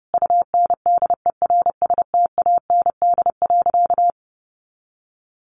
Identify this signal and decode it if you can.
Morse code, heard as a keyed tone, about 30 wpm, 700 Hz. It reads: UNDERSTAND.
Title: Sentences from Top 600 Words 30wpm